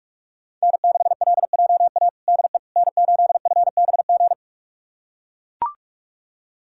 Morse code, heard as a keyed tone, about 43 wpm, 700 Hz. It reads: N6LJA DE N8FBG
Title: USA Callsign DE Callsign - 5 Characters - 45wpm